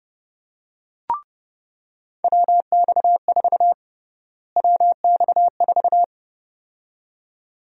Morse code, heard as a keyed tone, about 30 wpm, 700 Hz. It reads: WX4 WX4
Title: USA Call Sign Prefixes - Repeated 2x - 30wpm